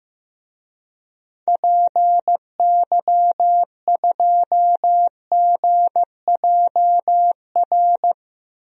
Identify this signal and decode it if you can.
Morse code, heard as a keyed tone, about 15 wpm, 700 Hz. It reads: PY2GJR
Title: DXCC-100 Call Signs - 6 Characters - 15wpm